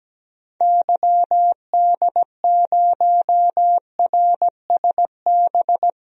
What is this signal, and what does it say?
Morse code, heard as a keyed tone, about 17 wpm, 700 Hz. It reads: YD0RSB
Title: DXCC-100 Call Signs - 6 Characters - Repeated 3x - 17wpm